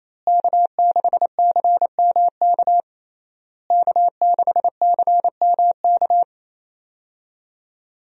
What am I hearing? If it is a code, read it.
Morse code, heard as a keyed tone, about 28 wpm, 700 Hz. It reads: K6CMK K6CMK
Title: USA Call Signs - 5 Characters - Repeated 3x - 28wpm